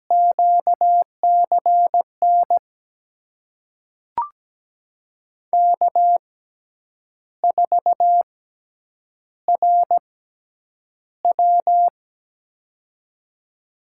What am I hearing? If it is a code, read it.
Morse code, heard as a keyed tone, about 17 wpm, spaced as for 8 wpm, 700 Hz. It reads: QCN K4RW
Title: USA Call Signs - 6x Character Spacing - 17wpm